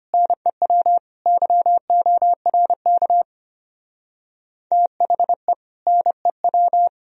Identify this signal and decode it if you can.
Morse code, heard as a keyed tone, about 29 wpm, 700 Hz. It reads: NEW YORK THE NEW
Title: Speed-Racing - Sets of 3 Words 25wpm